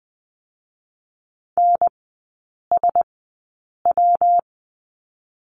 Morse code, heard as a keyed tone, about 20 wpm, 700 Hz. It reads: N S W
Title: 114 Lesson 19e - Letter w - Three Characters 20wpm